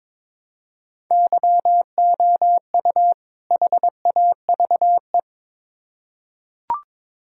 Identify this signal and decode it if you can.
Morse code, heard as a keyed tone, about 22 wpm, 700 Hz. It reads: YOU HAVE
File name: Build A Phrase - 2-Word Phrases - 22wpm